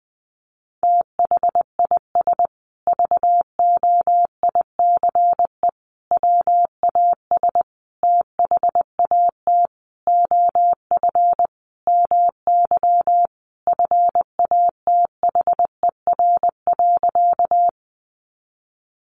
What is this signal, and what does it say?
Morse code, heard as a keyed tone, about 20 wpm, 700 Hz. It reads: THIS VOICE WAS THAT OF MY FATHER.
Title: Sentences from Top 500 Words 20wpm